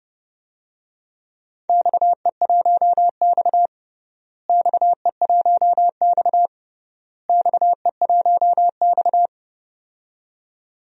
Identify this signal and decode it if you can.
Morse code, heard as a keyed tone, about 30 wpm, 700 Hz. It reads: XE1X XE1X XE1X
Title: DXCC-100 Call Signs - 4 Characters - Repeated 3x - 30wpm